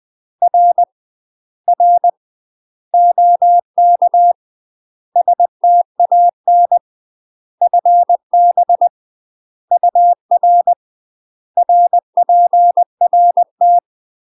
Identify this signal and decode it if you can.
Morse code, heard as a keyed tone, about 20 wpm, 700 Hz. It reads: R R OK STAN FB UR RPRT
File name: UK MASTER QSOs TRAINING / 20WPM / UK MASTER QSOs 0028